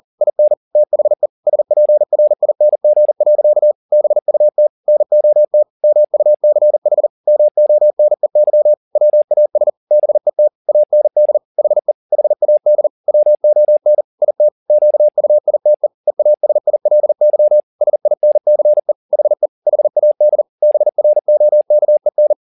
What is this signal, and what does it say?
Morse code, heard as a keyed tone, about 40 wpm, 600 Hz. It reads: IN THE SPRING. BUT NOT MUCH MONEY WAS BET AND HE HAD WON IT QUITE EASILY SINCE HE HAD BROKEN
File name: the old man and the sea 2 40wpm